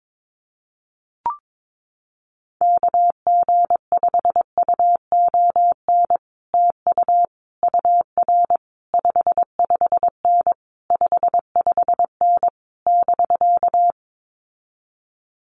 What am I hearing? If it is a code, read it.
Morse code, heard as a keyed tone, about 22 wpm, 700 Hz. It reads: KG5UON TU UR 55N 55N <BK>
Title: Common QSO Phrases Encore 22wpm